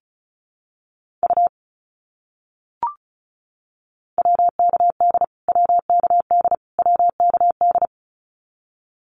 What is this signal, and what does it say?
Morse code, heard as a keyed tone, about 34 wpm, 700 Hz. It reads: U WKD WKD WKD
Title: Abbreviations Repeated 3x - 35wpm